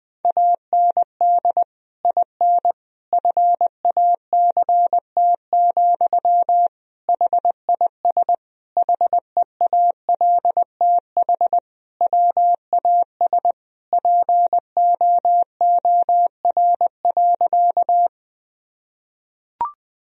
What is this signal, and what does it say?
Morse code, heard as a keyed tone, about 20 wpm, 700 Hz. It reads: AND IN FACT, HIS HEALTH WAS POOR.
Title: Sentences from Top 1000 Words - Review Part 3 - 20wpm